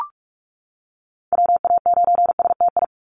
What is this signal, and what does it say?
Morse code, heard as a keyed tone, about 44 wpm, 700 Hz. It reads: WA9STI
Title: NA SOTA Activator Callsigns 45wpm